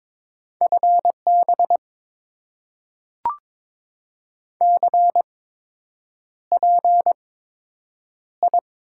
Morse code, heard as a keyed tone, about 22 wpm, spaced as for 8 wpm, 690 Hz. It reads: FB CPI
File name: Top 90 QSO Elements - 8x Character Spacing - 22wpm